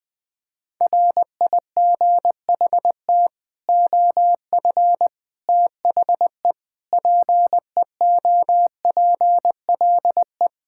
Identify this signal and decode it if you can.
Morse code, heard as a keyed tone, about 21 wpm, 700 Hz. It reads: RIGHT OF THE PEOPLE
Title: Sets of 8 Words - 20wpm